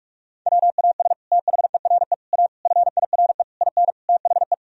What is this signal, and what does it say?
Morse code, heard as a keyed tone, about 42 wpm, 700 Hz. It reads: WAS THERE A FIRE IN THE
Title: Sentences from Top 1000 Words - Review Part 4 - 45wpm